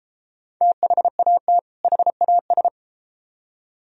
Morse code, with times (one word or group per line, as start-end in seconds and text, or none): 0.6-1.6 THAT
1.8-2.7 HAS